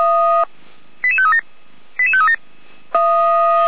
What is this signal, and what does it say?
Five-tone selective call (ZVEI), heard as quick runs of five tones, tones of about 70 ms. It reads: 88537 88537
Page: /455/SDR/ZVEI/